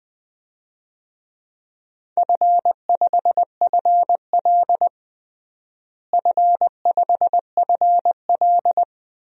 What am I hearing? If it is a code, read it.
Morse code, heard as a keyed tone, about 20 wpm, 700 Hz. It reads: F5FL F5FL
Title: DXCC-100 Call Signs - 4 Characters - Repeated 2x - 20wpm